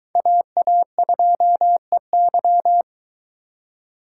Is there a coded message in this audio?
Yes: Morse code from 0.2 to 2.8 s, heard as a keyed tone, about 23 wpm, 700 Hz. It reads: AA2EY